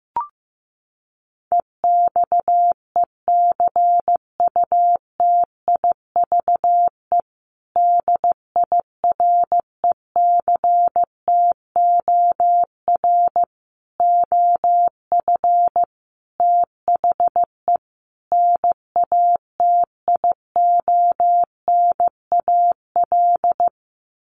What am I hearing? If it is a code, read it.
Morse code, heard as a keyed tone, about 15 wpm, 700 Hz. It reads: EXECUTIVE DIRECTOR OF THE NATIONAL
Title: Sets of 5 Words Encore2 15wpm